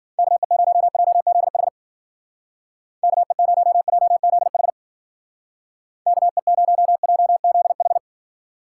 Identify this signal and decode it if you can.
Morse code, heard as a keyed tone, about 56 wpm, 700 Hz. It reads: KE0JZH KE0JZH KE0JZH